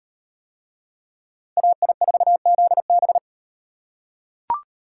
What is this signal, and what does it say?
Morse code, heard as a keyed tone, about 38 wpm, 700 Hz. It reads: AI4ZB